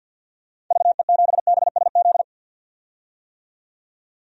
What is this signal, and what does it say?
Morse code, heard as a keyed tone, about 47 wpm, 700 Hz. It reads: VE7BSZ
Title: DXCC-100 Call Signs - 6 Characters - 50wpm